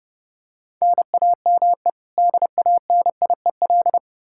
Morse code, heard as a keyed tone, about 30 wpm, 700 Hz. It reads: NAME DANIEL
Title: Practice QSO - Exchange 1 - Beginning - 30wpm